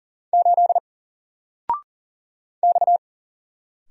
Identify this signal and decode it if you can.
Morse code, heard as a keyed tone, about 39 wpm, 700 Hz. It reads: 8 X
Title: Single Character (Letter:Number:Punctuation:Prosign) Alt 40wpm